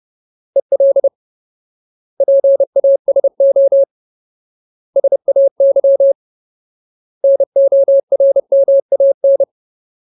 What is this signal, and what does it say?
Morse code, heard as a keyed tone, about 30 wpm, 550 Hz. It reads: EL PASO SAY NORMAN
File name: Ultimate Word List / 30WPM Ultimate